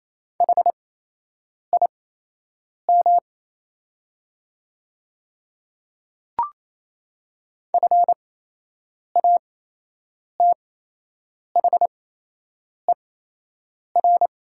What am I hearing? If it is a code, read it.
Morse code, heard as a keyed tone, about 28 wpm, spaced as for 8 wpm, 700 Hz. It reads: HIM FATHER
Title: 8x Character Spacing - Top 200 Words - 28wpm